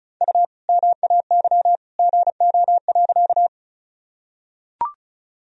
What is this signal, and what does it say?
Morse code, heard as a keyed tone, about 34 wpm, 700 Hz. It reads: U MAY GO.
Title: Sentences from Top 100 Words and Abbreviations 35wpm